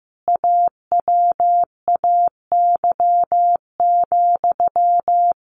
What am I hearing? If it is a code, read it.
Morse code, heard as a keyed tone, about 15 wpm, 700 Hz. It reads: AWAY,
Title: A Sound Of Thunder 15wpm